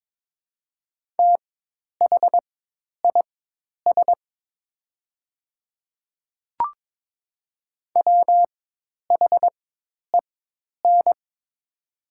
Morse code, heard as a keyed tone, about 22 wpm, spaced as for 10 wpm, 700 Hz. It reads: THIS WHEN
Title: 4x Character Spacing - Top 100 Words - 22wpm